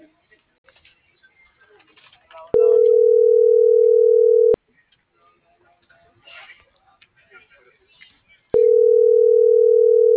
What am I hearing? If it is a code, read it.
Ringback tone (North America), 440 + 480 Hz.